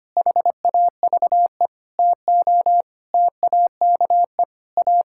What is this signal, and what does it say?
Morse code, heard as a keyed tone, about 25 wpm, 700 Hz. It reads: HAVE TO TAKE A
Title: Sentences from Top 1000 Words - Review Part 6 - 25wpm